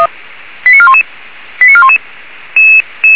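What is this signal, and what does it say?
Five-tone selective call (ZVEI), heard as quick runs of five tones, tones of about 70 ms. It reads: 89411 89411
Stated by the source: Received on an SDR as a ZVEI five-tone call.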